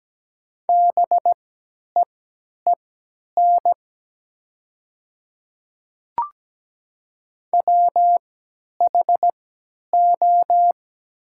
Morse code, heard as a keyed tone, about 17 wpm, spaced as for 10 wpm, 700 Hz. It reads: BEEN WHO
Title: 3x Character Spacing - Top 100 Words - 17wpm